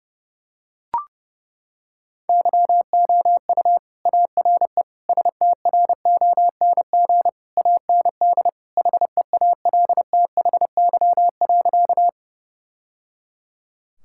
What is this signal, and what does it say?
Morse code, heard as a keyed tone, about 30 wpm, 700 Hz. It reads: YOU ARE STRONG AND HEALTHY.
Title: Sentences from Top 1000 Words - Review Part 5 - 30wpm